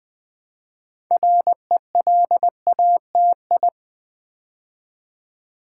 Morse code, heard as a keyed tone, about 20 wpm, 700 Hz. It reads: RELATI
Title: Build A Word - 3 Letters At A Time - 8-Letter Words - 20wpm